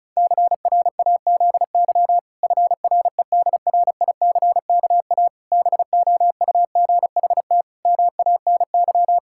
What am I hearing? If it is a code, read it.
Morse code, heard as a keyed tone, about 36 wpm, 700 Hz. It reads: CRAZY FREDRICKA BOUGHT MANY
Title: Pangram Challenge v2 - 35wpm